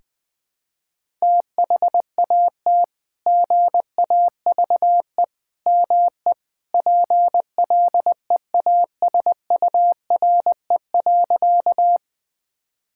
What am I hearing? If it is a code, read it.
Morse code, heard as a keyed tone, about 20 wpm, 700 Hz. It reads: THAT GAVE ME PLEASURE.
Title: Sentences from Top 1600 Words 20wpm